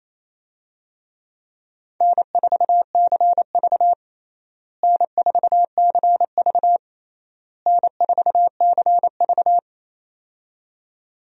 Morse code, heard as a keyed tone, about 28 wpm, 700 Hz. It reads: N4CV N4CV N4CV
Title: USA Call Signs - 4 Characters - Repeated 3x - 28wpm